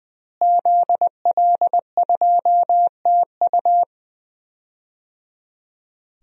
Morse code, heard as a keyed tone, about 20 wpm, 700 Hz. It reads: ZL2TU
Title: DXCC-100 Call Signs - 5 Characters - Repeated 3x - 20wpm